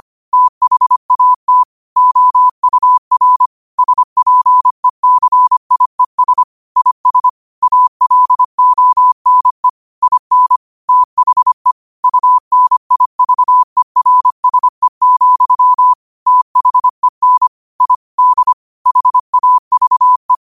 Morse code, heard as a keyed tone, about 25 wpm, 1000 Hz. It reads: THAT OUR SPECIES IS ALONE IN THE UNIVERSE, THEN I D HAVE
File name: 25wpm